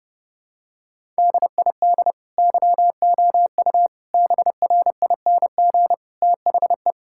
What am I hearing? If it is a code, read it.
Morse code, heard as a keyed tone, about 32 wpm, 700 Hz. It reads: DID YOU BRING THE
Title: Sentences from Top 1000 Words - Review Part 6 - 30wpm